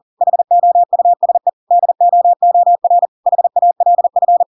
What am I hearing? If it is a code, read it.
Morse code, heard as a keyed tone, about 40 wpm, 700 Hz. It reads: HOUSE DOOR HALF
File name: Pinocchio / Pinocchio ch4 40 40 ik1hge